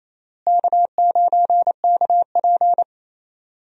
Morse code, heard as a keyed tone, about 28 wpm, 700 Hz. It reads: K9KP
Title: USA Call Signs - 4 Characters - Repeated 2x - 28wpm